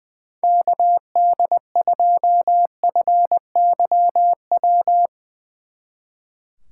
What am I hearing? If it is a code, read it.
Morse code, heard as a keyed tone, about 23 wpm, 700 Hz. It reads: KD2FYW
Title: Speed Racing - USA 2x3 Call Signs 20wpm